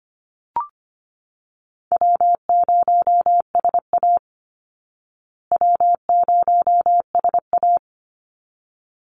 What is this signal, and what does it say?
Morse code, heard as a keyed tone, about 25 wpm, 700 Hz. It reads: W0SA W0SA
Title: Call Signs Repeated 3x - 25wpm